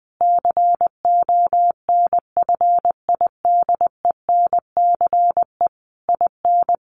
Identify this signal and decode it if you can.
Morse code, heard as a keyed tone, about 20 wpm, 700 Hz. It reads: CONFIDENCE IN
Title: Sentences from Top 1500 Words 20wpm